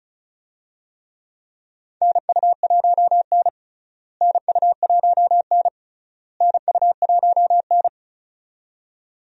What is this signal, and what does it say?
Morse code, heard as a keyed tone, about 34 wpm, 700 Hz. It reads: NU1N NU1N NU1N
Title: USA Call Signs - 4 Characters - Repeated 3x - 35wpm